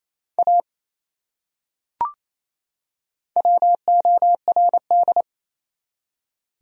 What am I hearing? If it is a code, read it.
Morse code, heard as a keyed tone, about 28 wpm, 700 Hz. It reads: A WORD
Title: Top 100 Words 28wpm